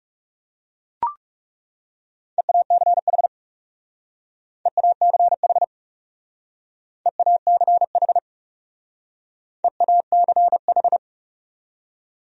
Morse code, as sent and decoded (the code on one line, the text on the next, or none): . .- -.-. .... / . .- -.-. .... / . .- -.-. .... / . .- -.-. ....
EACH EACH EACH EACH